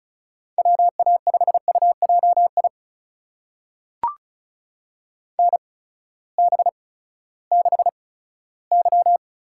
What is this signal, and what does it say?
Morse code, heard as a keyed tone, about 36 wpm, spaced as for 18 wpm, 700 Hz. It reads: WA5UJI NB6Y
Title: USA Call Signs - 8x Character Spacing - 35wpm